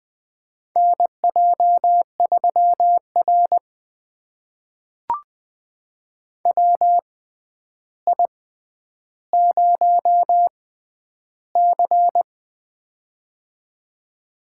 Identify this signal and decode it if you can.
Morse code, heard as a keyed tone, about 20 wpm, spaced as for 10 wpm, 700 Hz. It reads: NJ3R WI0C
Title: USA Call Signs - 4 Characters - 6x Character Spacing - 20wpm